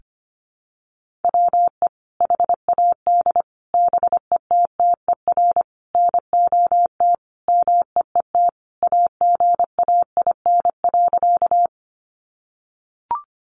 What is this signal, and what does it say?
Morse code, heard as a keyed tone, about 25 wpm, 700 Hz. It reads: WE HAD BETTER NOT MEET AGAIN.
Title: Sentences from Top 1100 Words 25wpm